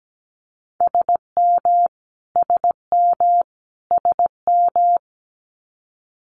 Morse code, heard as a keyed tone, about 17 wpm, 700 Hz. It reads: SM SM SM
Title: Multi-letter Phonemes Repeated 3x 17wpm